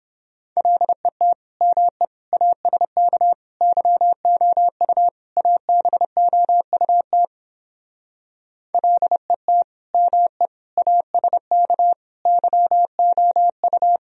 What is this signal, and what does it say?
Morse code, heard as a keyed tone, about 29 wpm, 700 Hz. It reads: LET ME ASK YOU ABOUT LET ME ASK YOU